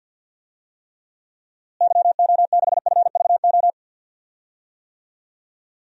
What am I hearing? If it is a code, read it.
Morse code, heard as a keyed tone, about 50 wpm, 700 Hz. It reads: YO6FVO
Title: DXCC-100 Call Signs - 6 Characters - Repeated 3x - 50wpm